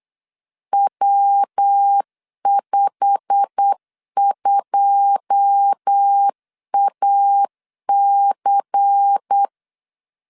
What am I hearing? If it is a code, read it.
Morse code, heard as a keyed tone, about 8 wpm, 800 Hz. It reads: W52AC
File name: LEZIONI / 9 2 8wpm 3437